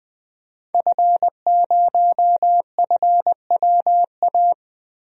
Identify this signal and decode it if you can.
Morse code, heard as a keyed tone, about 20 wpm, 700 Hz. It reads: F0FWA
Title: DXCC-100 Call Signs - 5 Characters - Repeated 2x - 20wpm